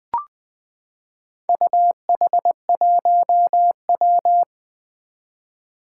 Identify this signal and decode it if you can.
Morse code, heard as a keyed tone, about 20 wpm, 700 Hz. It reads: UH1W
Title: Four Challenging Letter-Number Combinations 20wpm